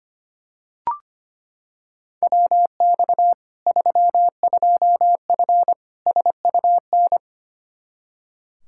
Morse code, heard as a keyed tone, about 25 wpm, 700 Hz. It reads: WX 32F SUN
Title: Practice QSO - Exchange 2 - Info - 25wpm